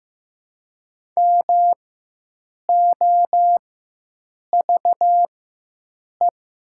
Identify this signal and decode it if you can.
Morse code, heard as a keyed tone, about 15 wpm, spaced as for 7 wpm, 700 Hz. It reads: MOVE
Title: 4x Character Spacing - Top 500 Words - 15wpm